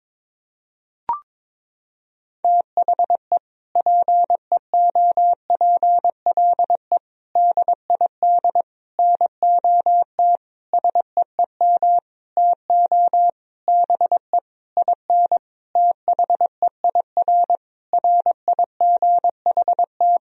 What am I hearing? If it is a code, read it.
Morse code, heard as a keyed tone, about 21 wpm, 700 Hz. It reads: THE PEOPLE DID NOT SEEM TO BE IN THEIR RIGHT
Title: Sentences from Top 1000 Words - Review Part 1 - 22wpm